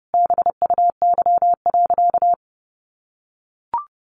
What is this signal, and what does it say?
Morse code, heard as a keyed tone, about 30 wpm, 700 Hz. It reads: BUY.
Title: Sentences from Top 1300 Words 30wpm